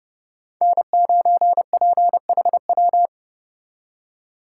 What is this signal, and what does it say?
Morse code, heard as a keyed tone, about 37 wpm, 700 Hz. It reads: N9PHW
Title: Speed Racing - USA 1x3 Call Signs 30wpm